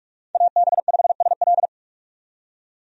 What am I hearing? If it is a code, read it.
Morse code, heard as a keyed tone, about 46 wpm, 700 Hz. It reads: AB5SL
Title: USA Call Signs - 5 Characters - 45wpm